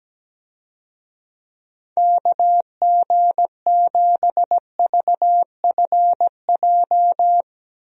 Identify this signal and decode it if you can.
Morse code, heard as a keyed tone, about 17 wpm, 700 Hz. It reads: KG7VFJ
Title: USA Call Signs - 6 Characters - Repeated 3x - 17wpm